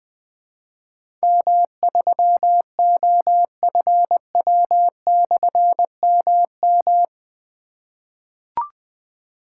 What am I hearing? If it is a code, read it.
Morse code, heard as a keyed tone, about 20 wpm, 700 Hz. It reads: M3OFW/MM
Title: Difficult Call Signs 20wpm